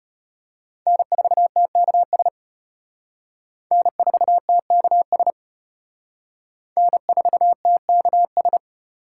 Morse code, heard as a keyed tone, about 32 wpm, 700 Hz. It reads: N4TKS N4TKS N4TKS